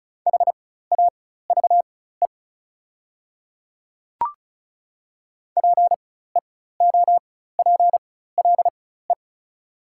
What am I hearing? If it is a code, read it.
Morse code, heard as a keyed tone, about 36 wpm, spaced as for 16 wpm, 690 Hz. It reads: HAVE PEOPLE
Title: 4x Character Spacing - Top 100 Words - 35wpm